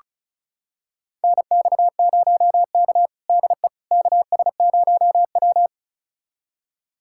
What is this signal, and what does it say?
Morse code, heard as a keyed tone, about 36 wpm, 700 Hz. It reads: NX0K DE KS0W
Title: USA Callsign DE Callsign - 4 Characters - 35wpm